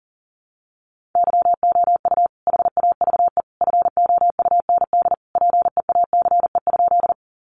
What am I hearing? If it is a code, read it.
Morse code, heard as a keyed tone, about 40 wpm, 700 Hz. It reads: YOU HAVE FOUND PEACE?
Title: Sentences from Top 1500 Words 40wpm